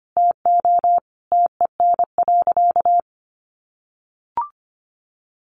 Morse code, heard as a keyed tone, about 25 wpm, 700 Hz. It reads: TO TEN.
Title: Easy Phrases 25wpm